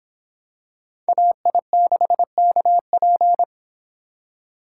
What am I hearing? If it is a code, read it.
Morse code, heard as a keyed tone, about 25 wpm, 700 Hz. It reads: AI6KP